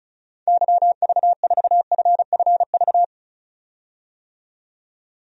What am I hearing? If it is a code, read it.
Morse code, heard as a keyed tone, about 36 wpm, 700 Hz. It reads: YV4FFV
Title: DXCC-100 Call Signs - 6 Characters - 35wpm